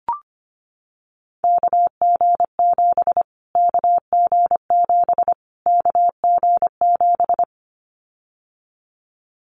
Morse code, heard as a keyed tone, about 25 wpm, 700 Hz. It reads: KG7 KG7 KG7
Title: Callsign Prefixes for Top 25 DXCC - Repeated 3x - 25wpm